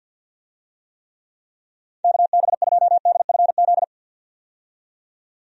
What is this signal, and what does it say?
Morse code, heard as a keyed tone, about 52 wpm, 700 Hz. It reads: KB2DFZ
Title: USA Call Signs - 6 Characters - 50wpm